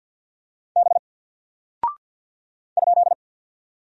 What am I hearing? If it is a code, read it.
Morse code, heard as a keyed tone, about 49 wpm, 700 Hz. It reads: B ?
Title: Single Character (Letter:Number:Punctuation:Prosign) Alt 50wpm